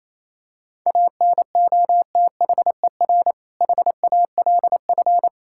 Morse code, heard as a keyed tone, about 28 wpm, 700 Hz. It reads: ANOTHER HALF
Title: Sentences from Top 1600 Words 28wpm